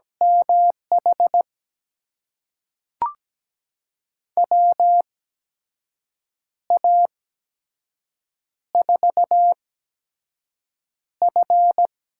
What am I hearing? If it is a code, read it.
Morse code, heard as a keyed tone, about 17 wpm, spaced as for 6 wpm, 700 Hz. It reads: MH WA4F
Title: USA Call Signs - 6 Characters - 8x Character Spacing - 17wpm